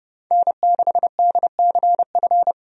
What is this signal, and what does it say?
Morse code, heard as a keyed tone, about 32 wpm, 700 Hz. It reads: N6DCF
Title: USA Call Signs - 5 Characters - 30wpm